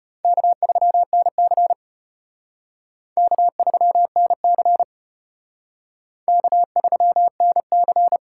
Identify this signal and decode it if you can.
Morse code, heard as a keyed tone, about 31 wpm, 700 Hz. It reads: K3NC K3NC K3NC
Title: Speed Racing - USA 1x2 Call Signs 25wpm